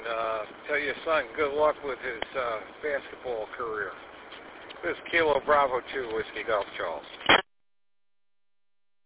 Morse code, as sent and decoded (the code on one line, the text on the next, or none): none